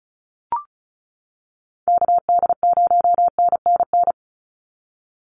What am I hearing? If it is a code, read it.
Morse code, heard as a keyed tone, about 34 wpm, 700 Hz. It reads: KD0NNN
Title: USA Call Signs 35wpm